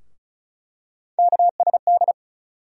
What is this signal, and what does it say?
Morse code, heard as a keyed tone, about 34 wpm, 700 Hz. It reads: KSD
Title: USA Call Sign Postfixes - 3 Letters 35wpm